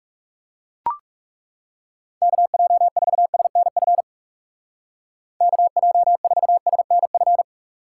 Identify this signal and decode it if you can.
Morse code, heard as a keyed tone, about 36 wpm, 700 Hz. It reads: KJ4SNF KJ4SNF
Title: Speed Racing - USA 2x3 Call Signs 30wpm